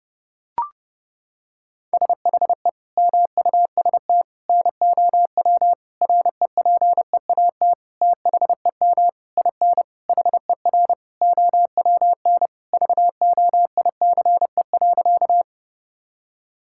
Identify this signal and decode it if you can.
Morse code, heard as a keyed tone, about 28 wpm, 700 Hz. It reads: SHE MUST NOW REPEAT THEM IN HER OWN VOICE.
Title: Sentences from Top 1600 Words 30wpm